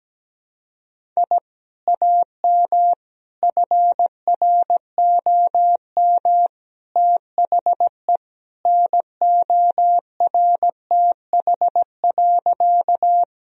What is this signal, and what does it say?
Morse code, heard as a keyed tone, about 17 wpm, 700 Hz. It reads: I AM FROM THE NORTH.
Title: Sentences from Top 1000 Words - Review Part 1 - 17wpm